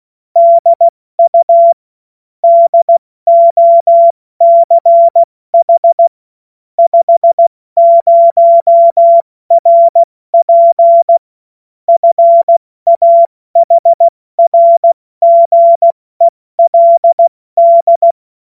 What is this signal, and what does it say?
Morse code, heard as a keyed tone, about 16 wpm, 680 Hz. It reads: DU DOCH 50 RP FAHRGELD
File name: Morsekurs / Witze / Witz 06. April 2021, 70 F80